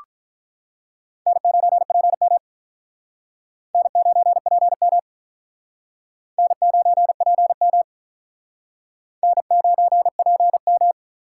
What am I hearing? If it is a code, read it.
Morse code, heard as a keyed tone, about 42 wpm, 700 Hz. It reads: N9PM N9PM N9PM N9PM
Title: Speed Racing - USA 1x2 Call Signs 35wpm